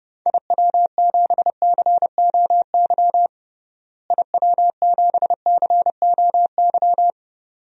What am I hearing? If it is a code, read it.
Morse code, heard as a keyed tone, about 30 wpm, 700 Hz. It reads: IW7COY IW7COY
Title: DXCC-100 Call Signs - 6 Characters - Repeated 2x - 30wpm